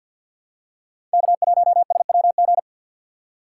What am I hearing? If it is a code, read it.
Morse code, heard as a keyed tone, about 53 wpm, 700 Hz. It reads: K1SWG
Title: USA Call Signs - 5 Characters - 8x Character Spacing - 50wpm